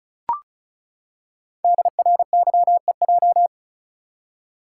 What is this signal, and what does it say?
Morse code, heard as a keyed tone, about 36 wpm, 700 Hz. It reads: DRYEJ
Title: Code Groups - 5 Letters - 35wpm